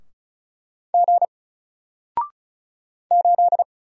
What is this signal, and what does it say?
Morse code, heard as a keyed tone, about 33 wpm, 700 Hz. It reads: G 8
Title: Single Letter-Number Alt 35wpm